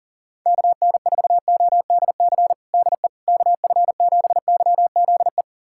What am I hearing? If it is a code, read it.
Morse code, heard as a keyed tone, about 40 wpm, 700 Hz. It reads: KN4ODC DE KF7YZE
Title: USA Callsign DE Callsign 40wpm